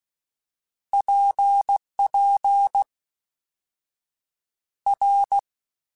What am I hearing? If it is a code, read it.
Morse code, heard as a keyed tone, about 16 wpm, 800 Hz. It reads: PP R
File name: Basic Letter Groups By7Groups / RKPX / RKPX@13